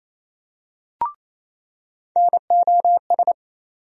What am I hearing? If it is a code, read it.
Morse code, heard as a keyed tone, about 28 wpm, 700 Hz. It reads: NOS
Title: USA Call Sign Postfixes - 3 Letters 28wpm